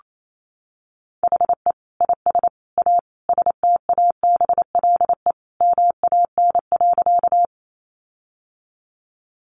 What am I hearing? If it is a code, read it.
Morse code, heard as a keyed tone, about 28 wpm, 700 Hz. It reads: HE IS A STABLE MAN.
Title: Sentences from Top 1500 Words 28wpm